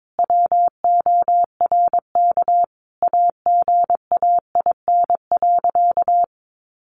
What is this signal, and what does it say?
Morse code, heard as a keyed tone, about 22 wpm, 700 Hz. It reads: WORK AGAIN.
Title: Sentences from Top 1500 Words 22wpm